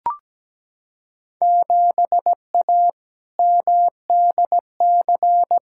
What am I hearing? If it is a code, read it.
Morse code, heard as a keyed tone, about 17 wpm, 700 Hz. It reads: 7A MDC
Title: Field Day Exchanges Encore - 17wpm